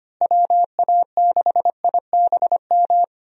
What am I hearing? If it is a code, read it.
Morse code, heard as a keyed tone, about 25 wpm, 700 Hz. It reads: WA6IBM
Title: USA Call Signs - 6 Characters - Repeated 2x - 25wpm